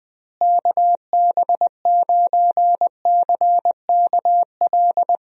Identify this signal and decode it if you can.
Morse code, heard as a keyed tone, about 20 wpm, 700 Hz. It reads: KB9CKL
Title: USA Call Signs - 6 Characters - Repeated 2x - 20wpm